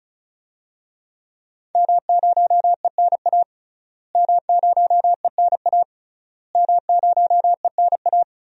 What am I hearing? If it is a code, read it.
Morse code, heard as a keyed tone, about 34 wpm, 700 Hz. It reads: M0ENA M0ENA M0ENA
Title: DXCC-100 Call Signs - 5 Characters - Repeated 3x - 35wpm